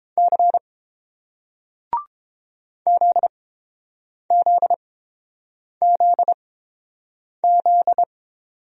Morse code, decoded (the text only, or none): C ZZZZ